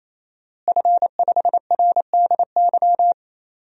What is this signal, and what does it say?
Morse code, heard as a keyed tone, about 28 wpm, 700 Hz. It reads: F5RDY
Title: DXCC-100 Call Signs - 5 Characters - Repeated 3x - 28wpm